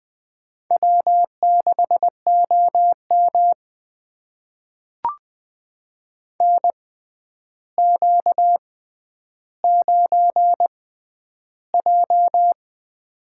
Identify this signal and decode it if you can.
Morse code, heard as a keyed tone, about 20 wpm, spaced as for 10 wpm, 700 Hz. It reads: W6OM NQ9J
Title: USA Call Signs - 4 Characters - 6x Character Spacing - 20wpm